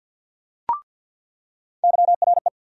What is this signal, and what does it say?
Morse code, heard as a keyed tone, about 47 wpm, 700 Hz. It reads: YRE
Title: USA Call Sign Postfixes - 3 Letters 50wpm